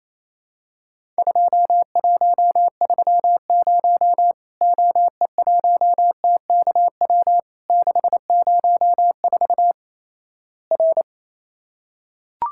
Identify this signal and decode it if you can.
Morse code, heard as a keyed tone, about 28 wpm, 700 Hz. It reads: 2130 OE1TKW 604 R
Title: Worked All Europe DX Contest - QTC Exchange - 28wpm